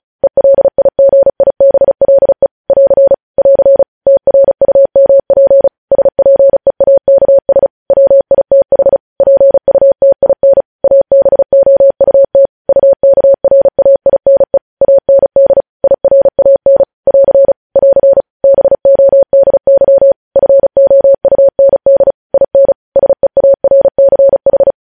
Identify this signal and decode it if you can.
Morse code, heard as a keyed tone, about 36 wpm, 550 Hz. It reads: ELIGIBLE <AR> <AR> TRUMP SPEAKS WITH PUTIN ABOUT UKRAINE AND IRAN <AR> <AR> BODY FOUND IN SEARCH
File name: podcast / News headlines-1x-35x35